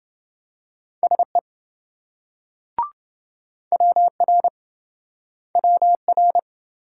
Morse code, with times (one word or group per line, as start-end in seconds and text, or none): 1.0-1.4 SE
3.7-4.5 WR
5.6-6.4 WR